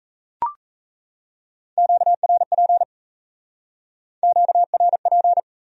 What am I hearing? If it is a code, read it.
Morse code, heard as a keyed tone, about 36 wpm, 700 Hz. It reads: QRP QRP